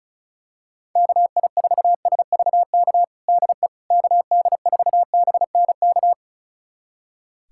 Morse code, heard as a keyed tone, about 36 wpm, 700 Hz. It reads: KI4SVK DE KD4BNK
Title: USA Callsign DE Callsign - 6 Characters - 35wpm